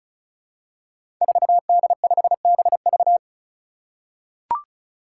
Morse code, heard as a keyed tone, about 36 wpm, 700 Hz. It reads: 4D5BV
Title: Five Challenging Letter-Number Combinations 35wpm